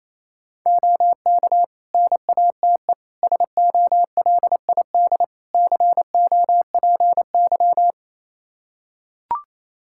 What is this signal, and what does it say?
Morse code, heard as a keyed tone, about 28 wpm, 700 Hz. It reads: OK NATE SOLID COPY
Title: Practice QSO - Exchange 3 - More Info - 28wpm